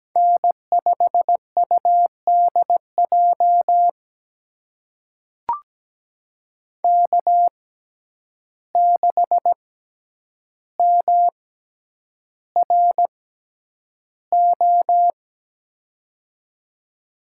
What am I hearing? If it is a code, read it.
Morse code, heard as a keyed tone, about 17 wpm, spaced as for 9 wpm, 700 Hz. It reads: N5UDJ K6MRO
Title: USA Call Signs - 6x Character Spacing - 17wpm